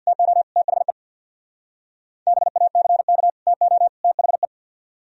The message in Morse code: TO THE BACK TO THE

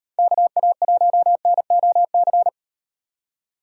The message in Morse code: KA1NOC

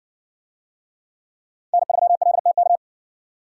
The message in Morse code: N3LTK

